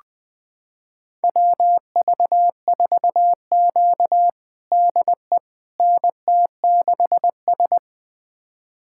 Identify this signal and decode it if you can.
Morse code, heard as a keyed tone, about 21 wpm, 700 Hz. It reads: WV4Q DE NT6S
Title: USA Callsign DE Callsign - 4 Characters - 20wpm